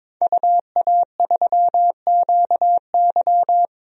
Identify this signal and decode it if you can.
Morse code, heard as a keyed tone, about 22 wpm, 700 Hz. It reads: UA3QY
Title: DXCC-100 Call Signs - 5 Characters - 22wpm